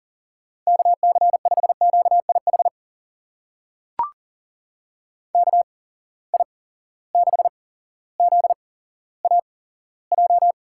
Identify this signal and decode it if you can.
Morse code, heard as a keyed tone, about 40 wpm, spaced as for 17 wpm, 700 Hz. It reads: KC5QIH KI6ZAJ
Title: USA Call Signs - 6 Characters - 8x Character Spacing - 40wpm